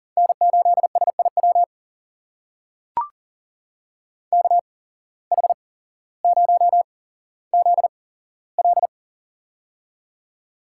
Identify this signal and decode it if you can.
Morse code, heard as a keyed tone, about 40 wpm, spaced as for 17 wpm, 700 Hz. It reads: N8SIW KH0ZL
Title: USA Call Signs - 5 Characters - 8x Character Spacing - 40wpm